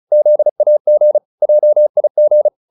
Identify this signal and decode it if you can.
Morse code, heard as a keyed tone, about 34 wpm, 600 Hz. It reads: ZAG JIG